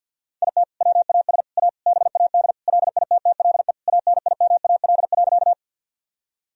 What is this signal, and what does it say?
Morse code, heard as a keyed tone, about 50 wpm, 700 Hz. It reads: IT WAS A BAD LITTLE ANIMAL.
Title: Sentences from Top 1000 Words - Review Part 1 - 50wpm